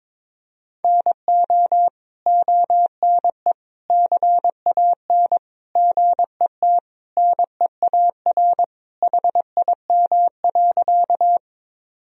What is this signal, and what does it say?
Morse code, heard as a keyed tone, about 22 wpm, 700 Hz. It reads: NO ONE CAN GET NEAR HIM.
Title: Sentences from Top 1000 Words - Review Part 1 - 22wpm